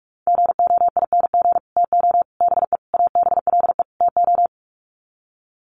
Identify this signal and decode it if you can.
Morse code, heard as a keyed tone, about 44 wpm, 700 Hz. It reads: GOING TO BE ABLE TO
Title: Sets of 5 Words 45wpm